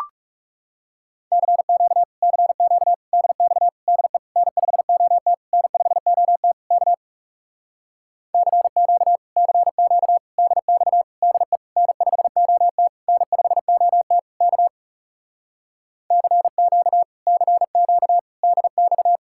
Morse code, heard as a keyed tone, about 40 wpm, 700 Hz. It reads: CQ CQ DX DE N5OT N5OT K CQ CQ DX DE N5OT N5OT K CQ CQ DX